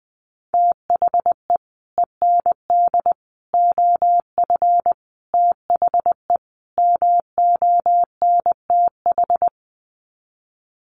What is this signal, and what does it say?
Morse code, heard as a keyed tone, about 20 wpm, 700 Hz. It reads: THE END OF THE MONTH
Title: Sets of 5 Words Encore 20wpm